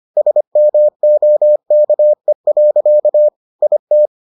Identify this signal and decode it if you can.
Morse code, heard as a keyed tone, about 25 wpm, 600 Hz. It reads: SMOKE. IT